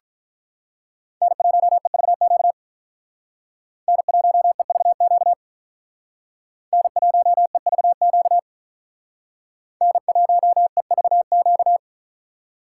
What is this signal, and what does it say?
Morse code, heard as a keyed tone, about 42 wpm, 700 Hz. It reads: N1EVQ N1EVQ N1EVQ N1EVQ
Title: Speed Racing - USA 1x3 Call Signs 35wpm